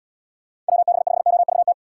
Morse code, heard as a keyed tone, about 78 wpm, 700 Hz. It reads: WB5OVT